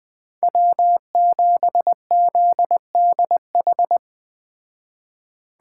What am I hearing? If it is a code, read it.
Morse code, heard as a keyed tone, about 20 wpm, 700 Hz. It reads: W7ZDH
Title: Speed Racing - USA Call Signs 17wpm